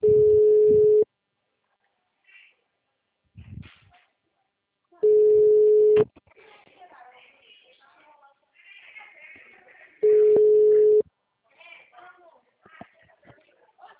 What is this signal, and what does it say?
Ringback tone (Europe, 425 Hz), with speech.